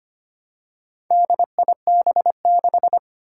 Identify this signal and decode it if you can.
Morse code, heard as a keyed tone, about 25 wpm, 700 Hz. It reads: DIB6
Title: Four Challenging Letter-Number Combinations 25wpm